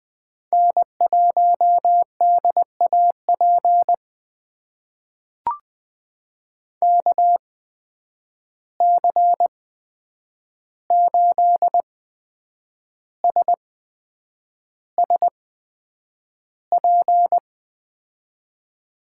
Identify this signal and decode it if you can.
Morse code, heard as a keyed tone, about 20 wpm, spaced as for 8 wpm, 700 Hz. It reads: N1DAP KC8SSP
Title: USA Call Signs - 8x Character Spacing - 20wpm